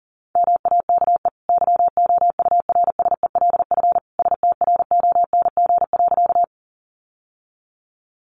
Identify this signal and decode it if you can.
Morse code, heard as a keyed tone, about 40 wpm, 700 Hz. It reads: MAKE YOURSELF STRONG.
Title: Sentences from Top 600 Words 40wpm